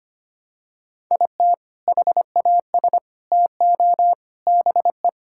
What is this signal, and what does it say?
Morse code, heard as a keyed tone, about 25 wpm, 700 Hz. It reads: IT HAS TO BE
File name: practice / 25wpm / Sentences with 3-Word Phrases - 25wpm